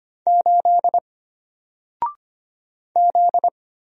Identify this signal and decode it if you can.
Morse code, heard as a keyed tone, about 25 wpm, 700 Hz. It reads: 8 Z
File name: Single-Letter-Number-25wpm